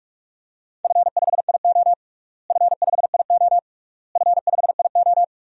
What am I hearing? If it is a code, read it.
Morse code, heard as a keyed tone, about 46 wpm, 700 Hz. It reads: F5IO F5IO F5IO
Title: DXCC-100 Call Signs - 4 Characters - Repeated 3x - 45wpm